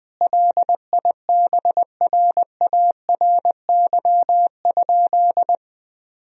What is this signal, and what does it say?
Morse code, heard as a keyed tone, about 21 wpm, 700 Hz. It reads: LIBRARY?
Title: Sentences from Top 1000 Words - Review Part 4 - 20wpm